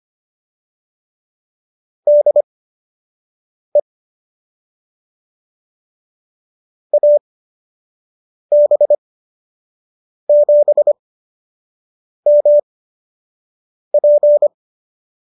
Morse code, heard as a keyed tone, about 25 wpm, spaced as for 6 wpm, 570 Hz. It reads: DE AB7MP